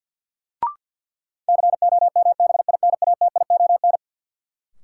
Morse code, heard as a keyed tone, about 52 wpm, 700 Hz. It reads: COMBINATION
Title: Sentences from Top 1000 Words - Review Part 6 - New Word Drill - 50wpm